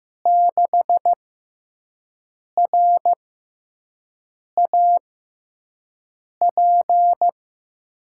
Morse code, heard as a keyed tone, about 15 wpm, spaced as for 5 wpm, 700 Hz. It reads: 6RAP
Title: USA Call Signs - 5 Characters - 6x Character Spacing - 15wpm